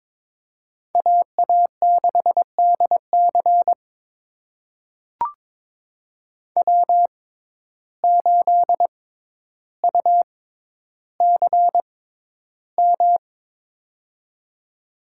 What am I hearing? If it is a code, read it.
Morse code, heard as a keyed tone, about 22 wpm, spaced as for 11 wpm, 700 Hz. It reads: AA6DC W8UCM
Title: USA Call Signs - 5 Characters - 6x Character Spacing - 22wpm